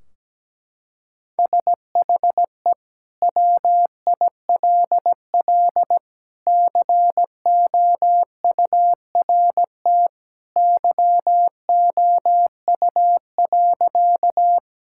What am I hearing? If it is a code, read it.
Morse code, heard as a keyed tone, about 17 wpm, 700 Hz. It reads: SHE WILL COURT YOU.
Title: Sentences from Top 1600 Words 17wpm